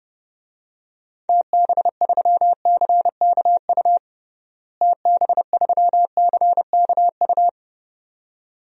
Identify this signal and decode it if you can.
Morse code, heard as a keyed tone, about 32 wpm, 700 Hz. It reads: TB3CKU TB3CKU
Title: DXCC-100 Call Signs - 6 Characters - Repeated 2x - 30wpm